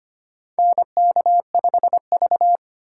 Morse code, heard as a keyed tone, about 25 wpm, 700 Hz. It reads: NK5V
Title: USA Call Signs - 4 Characters - 4x Character Spacing - 25wpm